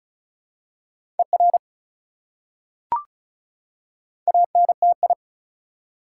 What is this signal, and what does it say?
Morse code, heard as a keyed tone, about 36 wpm, 700 Hz. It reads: ER ANTI
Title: Prefixes and Suffixes 35wpm